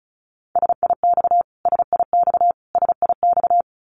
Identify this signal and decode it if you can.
Morse code, heard as a keyed tone, about 34 wpm, 700 Hz. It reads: SIX SIX SIX
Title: Top 500 Words Repeated 35wpm